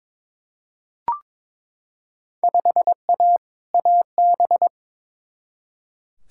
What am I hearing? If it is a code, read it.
Morse code, heard as a keyed tone, about 22 wpm, 700 Hz. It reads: 5A AB
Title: Field Day Exchanges Encore - 22wpm